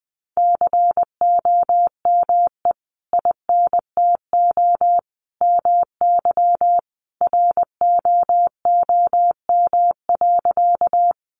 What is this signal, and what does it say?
Morse code, heard as a keyed tone, about 20 wpm, 700 Hz. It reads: COME INTO MY ROOM.
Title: Sentences from Top 600 Words 20wpm